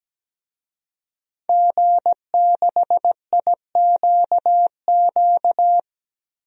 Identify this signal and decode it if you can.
Morse code, heard as a keyed tone, about 17 wpm, 700 Hz. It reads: G6IQQ
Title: DXCC-100 Call Signs - 5 Characters - 17wpm